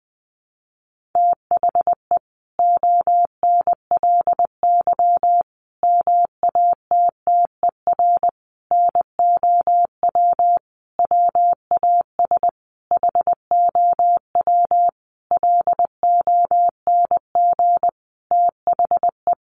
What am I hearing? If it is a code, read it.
Morse code, heard as a keyed tone, about 20 wpm, 700 Hz. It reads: THE ONLY MATTER NOW WAS HOW LONG THE
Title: Sentences from Top 1400 Words 20wpm